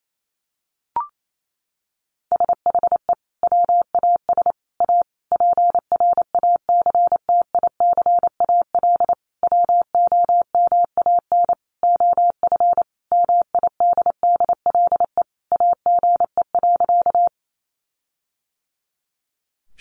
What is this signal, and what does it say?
Morse code, heard as a keyed tone, about 28 wpm, 700 Hz. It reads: SHE WAS A PRACTICAL WOMAN OF MIDDLE AGE.
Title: Sentences from Top 1400 Words 28wpm